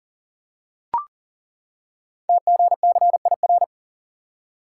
Morse code, heard as a keyed tone, about 39 wpm, 700 Hz. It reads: TGCIR
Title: Code Groups - 5 Letters - 40wpm